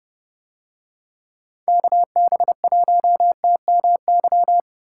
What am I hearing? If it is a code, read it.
Morse code, heard as a keyed tone, about 28 wpm, 700 Hz. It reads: KB1TMY